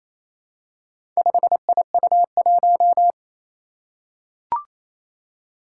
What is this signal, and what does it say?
Morse code, heard as a keyed tone, about 27 wpm, 700 Hz. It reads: 5IU1
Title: Four Challenging Letter-Number Combinations 28wpm